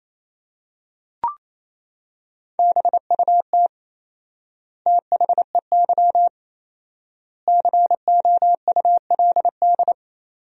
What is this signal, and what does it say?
Morse code, heard as a keyed tone, about 28 wpm, 700 Hz. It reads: BUT THEY COULD
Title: 4x Word-Spacing - Sentences from Top 500 Words - 28wpm